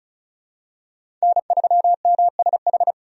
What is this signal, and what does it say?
Morse code, heard as a keyed tone, about 34 wpm, 700 Hz. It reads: N3MSH
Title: USA Call Signs - 5 Characters - 35wpm